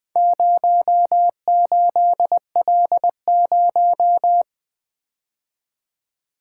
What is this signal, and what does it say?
Morse code, heard as a keyed tone, about 23 wpm, 700 Hz. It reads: 08L0
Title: Speed-Racing - Four Letter-Number Combinations - 20wpm